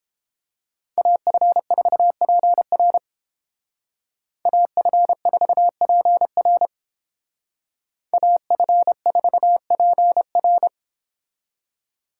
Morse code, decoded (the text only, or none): AF4PR AF4PR AF4PR